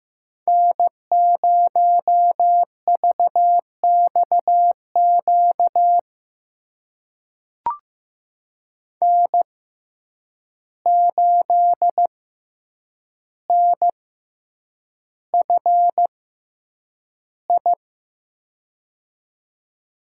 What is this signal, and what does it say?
Morse code, heard as a keyed tone, about 15 wpm, spaced as for 8 wpm, 700 Hz. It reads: N0VXQ N8NFI
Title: USA Call Signs - 5 Characters - 6x Character Spacing - 15wpm